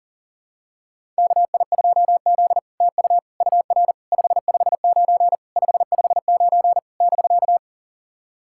Morse code, heard as a keyed tone, about 40 wpm, 700 Hz. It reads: KI2Z TU UR 559 559 <BK>
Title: SOTA QSO Practice 40wpm